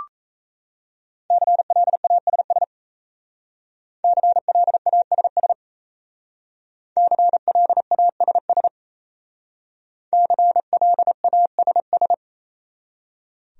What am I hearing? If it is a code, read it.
Morse code, heard as a keyed tone, about 36 wpm, 700 Hz. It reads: CLASS CLASS CLASS CLASS